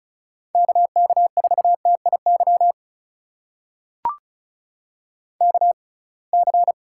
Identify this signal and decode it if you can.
Morse code, heard as a keyed tone, about 34 wpm, spaced as for 26 wpm, 700 Hz. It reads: KK4TIY KC
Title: USA Call Signs - 6 Characters - 6x Character Spacing - 35wpm